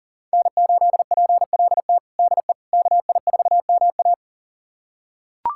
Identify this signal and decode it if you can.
Morse code, heard as a keyed tone, about 40 wpm, 700 Hz. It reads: N8PLT DE KI4MA
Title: USA Callsign DE Callsign - 5 Characters - 40wpm